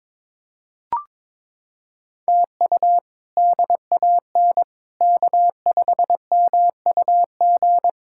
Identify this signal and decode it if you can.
Morse code, heard as a keyed tone, about 22 wpm, 700 Hz. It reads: TU DAN K5MUG
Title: K1USN SST QSO Practice - 22wpm